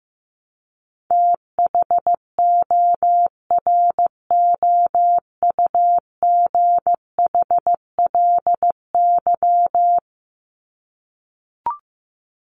Morse code, heard as a keyed tone, about 15 wpm, 700 Hz. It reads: THOROUGHLY
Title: LY Suffix - Top 100 Words with Base Words 15wpm